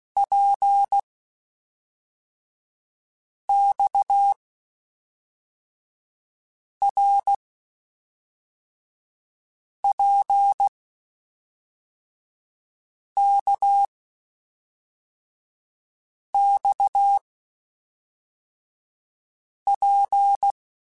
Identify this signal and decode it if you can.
Morse code, heard as a keyed tone, about 16 wpm, 800 Hz. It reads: P X R P K X P
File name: Basic Sets / Basic Letter Groups Collective7 / RKPX@5